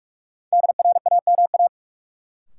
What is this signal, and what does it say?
Morse code, heard as a keyed tone, about 46 wpm, 700 Hz. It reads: DRAMA
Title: Sentences from Top 1500 Words - New Word Drill - 45wpm